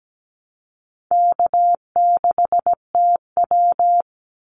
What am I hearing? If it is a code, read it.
Morse code, heard as a keyed tone, about 17 wpm, 700 Hz. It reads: K6TW
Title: NA SOTA Activator Callsigns 17wpm